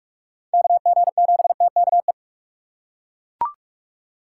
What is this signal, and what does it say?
Morse code, heard as a keyed tone, about 41 wpm, 700 Hz. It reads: KG7TKE